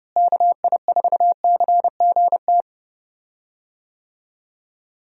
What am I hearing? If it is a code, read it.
Morse code, heard as a keyed tone, about 32 wpm, 700 Hz. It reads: KI4CGT
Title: USA Call Signs - 6 Characters - Repeated 2x - 30wpm